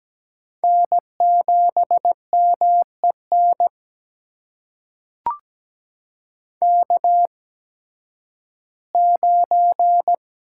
Morse code, heard as a keyed tone, about 17 wpm, spaced as for 11 wpm, 700 Hz. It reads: N7MEN K9
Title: USA Call Signs - 5 Characters - 8x Character Spacing - 17wpm